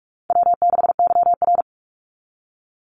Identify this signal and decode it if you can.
Morse code, heard as a keyed tone, about 44 wpm, 700 Hz. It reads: W6YR
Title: Easy Call Signs Repeated 3x - 45wpm